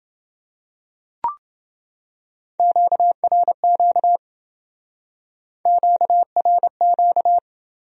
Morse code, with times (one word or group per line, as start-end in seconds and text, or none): 2.6-4.2 QRQ
5.7-7.4 QRQ